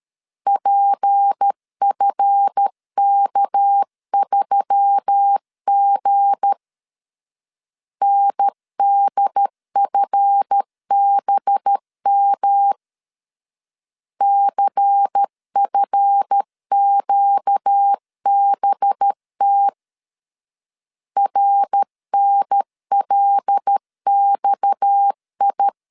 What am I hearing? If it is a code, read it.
Morse code, heard as a keyed tone, about 13 wpm, 800 Hz. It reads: PFK3G NDFBM CFQBT RNLXI
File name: LEZIONI / 7 GF9Z / 7 GF 12wpm 2122